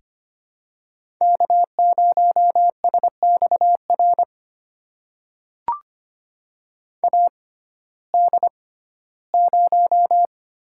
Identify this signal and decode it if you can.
Morse code, heard as a keyed tone, about 25 wpm, spaced as for 15 wpm, 700 Hz. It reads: K0SXR AD0
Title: USA Call Signs - 5 Characters - 6x Character Spacing - 25wpm